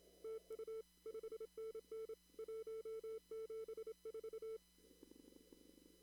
Morse code, heard as a keyed tone, about 26 wpm, 440 Hz. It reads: TU 5NN 174
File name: CW / kt5j wpx cw-op-k5pi